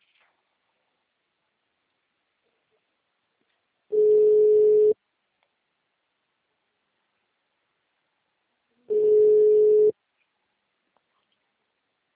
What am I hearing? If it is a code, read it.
Ringback tone (Europe, 425 Hz).
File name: out- -2024- - - .90189